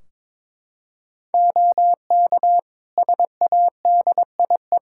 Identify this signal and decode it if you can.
Morse code, heard as a keyed tone, about 22 wpm, 700 Hz. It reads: OK SADIE
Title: Practice QSO - Exchange 4 - Ending - 22wpm